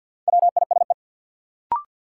Morse code, heard as a keyed tone, about 52 wpm, 700 Hz. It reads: WISE
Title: Sentences from Top 1600 Words - New Word Drill - 50wpm